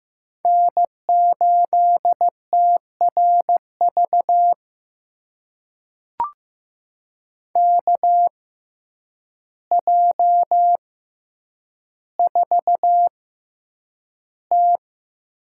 Morse code, heard as a keyed tone, about 15 wpm, spaced as for 8 wpm, 700 Hz. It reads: N8TRV KJ4T
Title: USA Call Signs - 6x Character Spacing - 15wpm